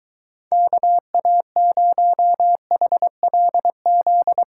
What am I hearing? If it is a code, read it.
Morse code, heard as a keyed tone, about 23 wpm, 700 Hz. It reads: KA0HLZ